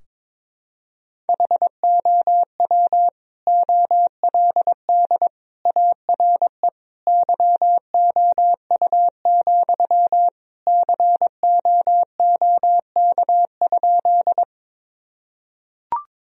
Morse code, heard as a keyed tone, about 22 wpm, 700 Hz. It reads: HOW OLD ARE YOU, COOK?
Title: Sentences from Top 1000 Words - Review Part 4 - 22wpm